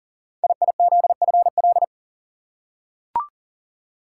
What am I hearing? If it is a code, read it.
Morse code, heard as a keyed tone, about 42 wpm, 700 Hz. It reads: IIZFL
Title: Code Groups - 5 Letters - 40wpm